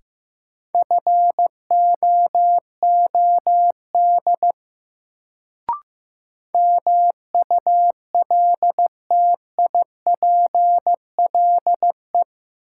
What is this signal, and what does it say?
Morse code, heard as a keyed tone, about 15 wpm, 700 Hz. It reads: FOOD MULTIPLE
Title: Sets of 5 Words - New Word Drill - 15wpm